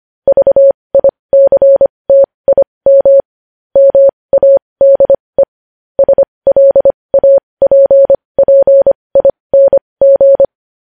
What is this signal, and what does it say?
Morse code, heard as a keyed tone, about 25 wpm, spaced as for 20 wpm, 550 Hz. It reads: VICTIM MADE SLAPPING
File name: podcast / News headlines-1x-25x20